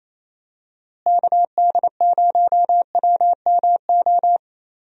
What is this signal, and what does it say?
Morse code, heard as a keyed tone, about 27 wpm, 700 Hz. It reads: KD0WMO
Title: USA Call Signs - 6 Characters - Repeated 3x - 28wpm